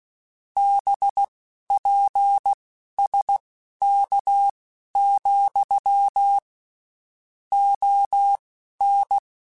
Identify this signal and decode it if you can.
Morse code, heard as a keyed tone, about 16 wpm, spaced as for 12 wpm, 800 Hz. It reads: BPSK, ON
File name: (spaced) / QSOs Spaced-mp3s / kc7sac @15 sp